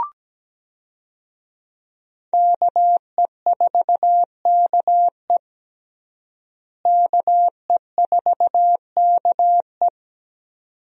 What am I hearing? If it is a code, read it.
Morse code, heard as a keyed tone, about 17 wpm, 700 Hz. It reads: KE4KE KE4KE
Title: USA Call Signs - 5 Characters - Repeated 2x - 17wpm